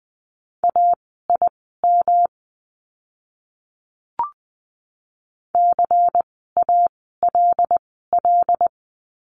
Morse code, heard as a keyed tone, about 20 wpm, spaced as for 14 wpm, 700 Hz. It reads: AIM CALL
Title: 108 Lesson 18f - Letter m - Words 20wpm